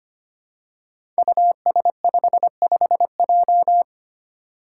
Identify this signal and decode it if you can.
Morse code, heard as a keyed tone, about 25 wpm, 700 Hz. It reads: US55J
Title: Five Challenging Letter-Number Combinations 25wpm